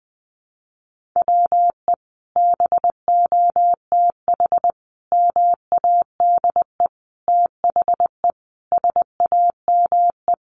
Morse code, heard as a keyed tone, about 20 wpm, 700 Hz. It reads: WE BOTH MADE THE SAME
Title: Sentences from Top 1500 Words 20wpm